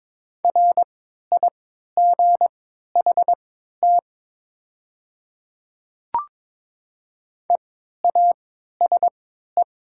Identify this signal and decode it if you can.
Morse code, heard as a keyed tone, about 22 wpm, spaced as for 12 wpm, 700 Hz. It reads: RIGHT EASE
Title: 3x Character Spacing - Top 300 Words - 22wpm